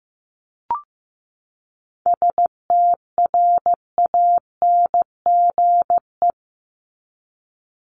Morse code, heard as a keyed tone, about 15 wpm, 700 Hz. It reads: STRANGE
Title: Top 1501-1750 Words 15wpm